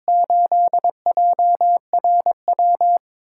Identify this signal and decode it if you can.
Morse code, heard as a keyed tone, about 22 wpm, 700 Hz. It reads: 8JRW
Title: USA Call Signs - 6x Character Spacing - 22wpm